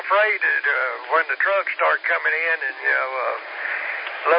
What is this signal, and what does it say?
Emergency Alert System (SAME) tones, heard as harsh buzzing bursts, with speech. It is heard from 0.3 to 4.1 s.